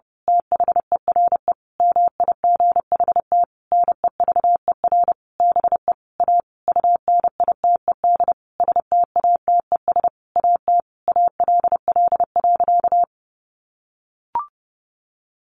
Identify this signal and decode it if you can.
Morse code, heard as a keyed tone, about 30 wpm, 700 Hz. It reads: THERE MIGHT NEVER BE A UNITED STATES AT ALL.
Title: A Sound Of Thunder 30wpm